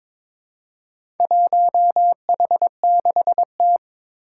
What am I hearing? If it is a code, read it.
Morse code, heard as a keyed tone, about 22 wpm, 700 Hz. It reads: 1H6T
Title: Four Challenging Letter-Number Combinations 22wpm